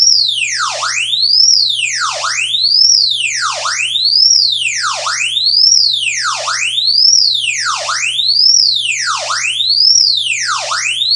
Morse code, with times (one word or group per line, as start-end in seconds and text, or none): none